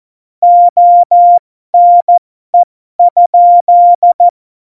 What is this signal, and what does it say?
Morse code, heard as a keyed tone, about 14 wpm, spaced as for 12 wpm, 700 Hz. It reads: ONE?
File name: Morsekurs / Witze